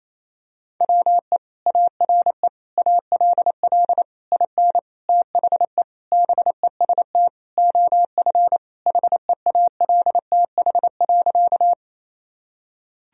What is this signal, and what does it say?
Morse code, heard as a keyed tone, about 28 wpm, 700 Hz. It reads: WE ARE ALL IN THE BEST OF HEALTH.
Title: Sentences from Top 1000 Words - Review Part 3 - 28wpm